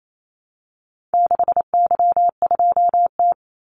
Morse code, heard as a keyed tone, about 28 wpm, 700 Hz. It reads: 6Y2T
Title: Call Signs Repeated 3x - 28wpm